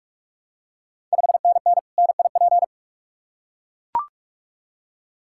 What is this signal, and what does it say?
Morse code, heard as a keyed tone, about 41 wpm, 700 Hz. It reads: 5NN NIP
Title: Ontario QSO Party Exchange 45wpm